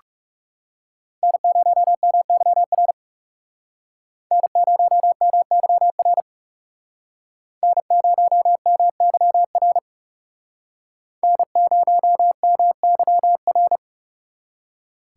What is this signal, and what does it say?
Morse code, heard as a keyed tone, about 38 wpm, 700 Hz. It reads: N0MYR N0MYR N0MYR N0MYR